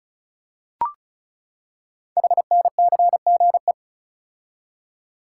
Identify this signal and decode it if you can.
Morse code, heard as a keyed tone, about 36 wpm, 700 Hz. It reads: HNCGE
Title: Code Groups - 5 Letters - 35wpm